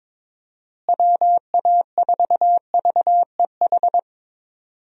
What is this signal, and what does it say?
Morse code, heard as a keyed tone, about 22 wpm, 700 Hz. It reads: WA4VEH
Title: USA Call Signs - 6 Characters - Repeated 2x - 22wpm